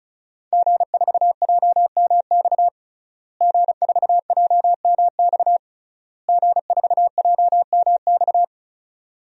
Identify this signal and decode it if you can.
Morse code, heard as a keyed tone, about 36 wpm, 700 Hz. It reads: G4JMX G4JMX G4JMX
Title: DXCC-100 Call Signs - 5 Characters - Repeated 3x - 35wpm